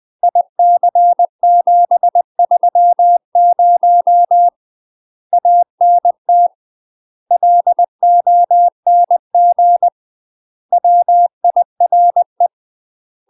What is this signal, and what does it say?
Morse code, heard as a keyed tone, about 20 wpm, 700 Hz. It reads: IC730 ANT LONG WIRE
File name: UK MASTER QSOs TRAINING / 20WPM / UK MASTER QSOs 0051